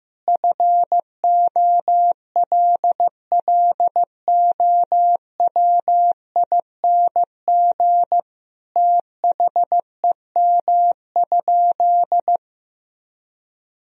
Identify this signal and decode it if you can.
Morse code, heard as a keyed tone, about 15 wpm, 700 Hz. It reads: FOLLOWING THEM?
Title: Sentences from Top 1000 Words - Review Part 4 - 15wpm